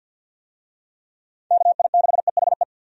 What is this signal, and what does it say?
Morse code, heard as a keyed tone, about 52 wpm, 700 Hz. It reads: KI6EHE
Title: USA Call Signs - 6 Characters - 50wpm